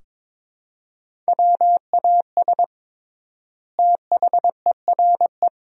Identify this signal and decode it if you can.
Morse code, heard as a keyed tone, about 22 wpm, 700 Hz. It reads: WAS THERE
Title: 3x Word-Spacing - Sentences from Top 400 Words - 22wpm